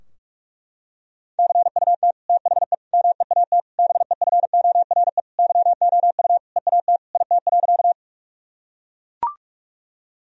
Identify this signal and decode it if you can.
Morse code, heard as a keyed tone, about 41 wpm, 700 Hz. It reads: CUT THE MEAT BEFORE YOU EAT IT.
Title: Sentences from Top 1000 Words - Review Part 3 - 45wpm